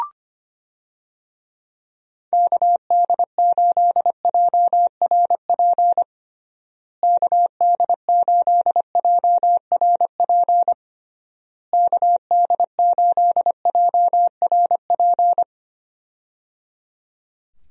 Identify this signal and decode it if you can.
Morse code, heard as a keyed tone, about 25 wpm, 700 Hz. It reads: KD8JRP KD8JRP KD8JRP
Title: USA Call Signs - 6 Characters - Repeated 3x - 25wpm